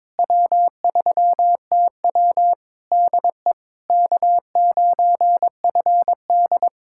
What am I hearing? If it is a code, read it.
Morse code, heard as a keyed tone, about 22 wpm, 700 Hz. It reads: W3TW DE K9FD
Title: Practice QSO - Exchange 4 - Ending - 22wpm